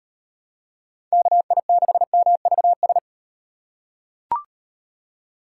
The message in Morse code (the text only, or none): KI6MVS